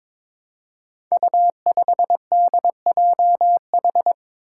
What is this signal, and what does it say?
Morse code, heard as a keyed tone, about 22 wpm, 700 Hz. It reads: U5DJH
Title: Five Challenging Letter-Number Combinations 22wpm